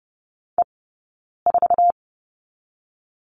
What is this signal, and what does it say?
Morse code, heard as a keyed tone, about 30 wpm, 700 Hz. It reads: E 4
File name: Lesson 17d - Letter u - Two Characters 30wpm